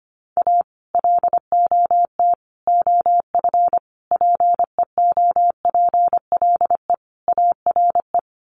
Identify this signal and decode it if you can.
Morse code, heard as a keyed tone, about 25 wpm, 700 Hz. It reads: A LOT OF PEOPLE ARE
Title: Sets of 5 Words 25wpm